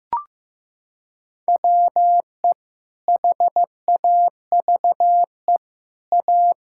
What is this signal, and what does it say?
Morse code, heard as a keyed tone, about 15 wpm, 700 Hz. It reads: WE HAVE A
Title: Sentences from Top 1000 Words - Review Part 6 - 15wpm